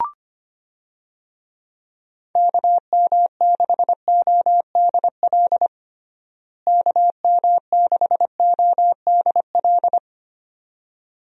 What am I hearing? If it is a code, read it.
Morse code, heard as a keyed tone, about 25 wpm, 700 Hz. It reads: KM6ODL KM6ODL
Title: USA Call Signs - 6 Characters - Repeated 2x - 25wpm